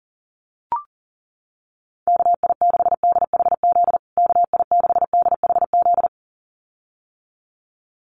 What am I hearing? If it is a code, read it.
Morse code, heard as a keyed tone, about 40 wpm, 700 Hz. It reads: KI6DHZ KI6DHZ
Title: USA Call Signs Repeated 2x 40wpm